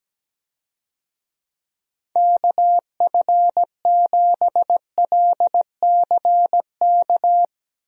Morse code, heard as a keyed tone, about 17 wpm, 700 Hz. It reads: KF7LCK
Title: USA Call Signs - 6 Characters - Repeated 3x - 17wpm